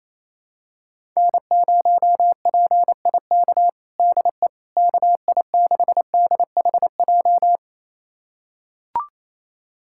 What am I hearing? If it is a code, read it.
Morse code, heard as a keyed tone, about 28 wpm, 700 Hz. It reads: N0PIK DE KI6DHJ
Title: USA Callsign DE Callsign 28wpm